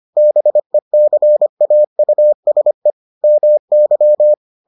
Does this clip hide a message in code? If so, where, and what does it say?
Yes: Morse code from 0.2 to 4.3 s, heard as a keyed tone, about 25 wpm, 600 Hz. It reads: BECAUSE MY